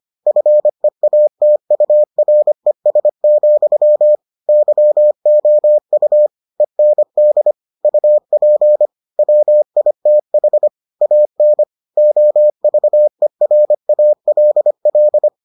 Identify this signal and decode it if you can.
Morse code, heard as a keyed tone, about 25 wpm, 600 Hz. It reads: FEATURES, YOU END UP WITH AN OVERALL